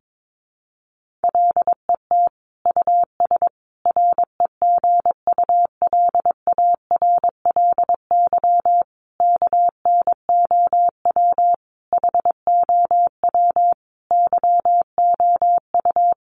Morse code, heard as a keyed tone, about 22 wpm, 700 Hz. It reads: LET US REGULARLY KNOW HOW YOU
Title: Sentences from Top 1400 Words 22wpm